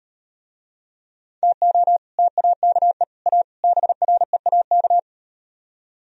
Morse code, heard as a keyed tone, about 38 wpm, 700 Hz. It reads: TO TAKE A BREAK